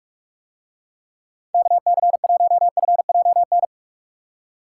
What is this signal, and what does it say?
Morse code, heard as a keyed tone, about 47 wpm, 700 Hz. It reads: KC1FJN